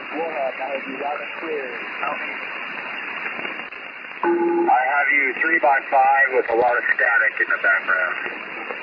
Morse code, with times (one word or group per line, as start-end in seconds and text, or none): none